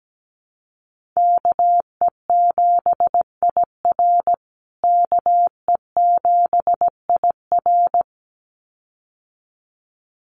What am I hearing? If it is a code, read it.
Morse code, heard as a keyed tone, about 17 wpm, 700 Hz. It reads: KE7IR KE7IR
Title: USA Call Signs Repeated 2x 17wpm